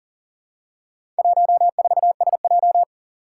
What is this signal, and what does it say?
Morse code, heard as a keyed tone, about 40 wpm, 700 Hz. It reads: 14SJ
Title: Four Challenging Letter-Number Combinations 40wpm